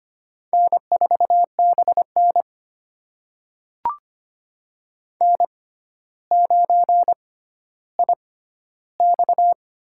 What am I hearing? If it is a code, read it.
Morse code, heard as a keyed tone, about 25 wpm, spaced as for 13 wpm, 700 Hz. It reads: N4BN N9IX
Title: USA Call Signs - 4 Characters - 6x Character Spacing - 25wpm